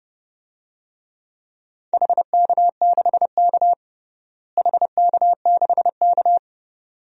Morse code, heard as a keyed tone, about 30 wpm, 700 Hz. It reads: HK6K HK6K
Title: DXCC-100 Call Signs - 4 Characters - Repeated 2x - 30wpm